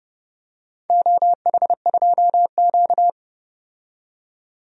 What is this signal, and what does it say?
Morse code, heard as a keyed tone, about 30 wpm, 700 Hz. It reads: OH2Q
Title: DXCC-100 Call Signs - 4 Characters - 30wpm